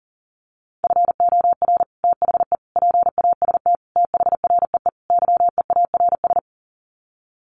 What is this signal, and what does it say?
Morse code, heard as a keyed tone, about 40 wpm, 700 Hz. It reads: FOR THE PAST THREE YEARS
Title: Sets of 5 Words Encore2 40wpm